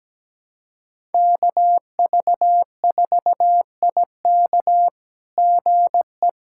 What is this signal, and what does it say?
Morse code, heard as a keyed tone, about 17 wpm, 700 Hz. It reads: KV4IK GE
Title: SOTA QSO Practice 17wpm